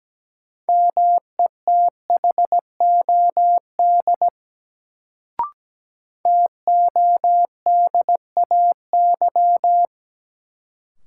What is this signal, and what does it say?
Morse code, heard as a keyed tone, about 17 wpm, 700 Hz. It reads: METHOD TODAY
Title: Sentences from Top 600 Words - New Word Drill - 17wpm